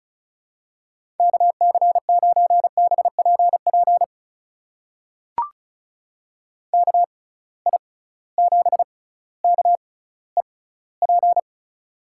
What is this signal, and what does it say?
Morse code, heard as a keyed tone, about 36 wpm, spaced as for 18 wpm, 700 Hz. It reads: KC9BPP KI7KEP
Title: USA Call Signs - 6 Characters - 6x Character Spacing - 35wpm